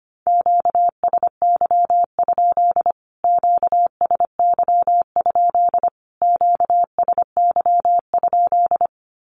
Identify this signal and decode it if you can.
Morse code, heard as a keyed tone, about 25 wpm, 700 Hz. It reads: QSY? QSY? QSY?
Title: Q-Codes Repeated 3x 25wpm